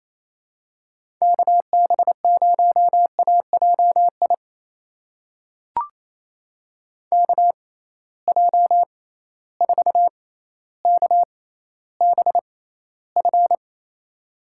Morse code, heard as a keyed tone, about 28 wpm, spaced as for 14 wpm, 700 Hz. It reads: KB0AJI KJ4KBF
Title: USA Call Signs - 6 Characters - 6x Character Spacing - 28wpm